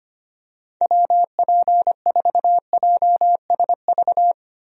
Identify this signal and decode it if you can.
Morse code, heard as a keyed tone, about 25 wpm, 700 Hz. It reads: WP4JSV
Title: USA Call Signs - 6 Characters - 25wpm